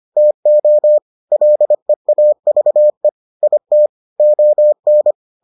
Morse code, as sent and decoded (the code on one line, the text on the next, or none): - --- / .-.. .- ...- . / .. - / --- -.
TO LEAVE IT ON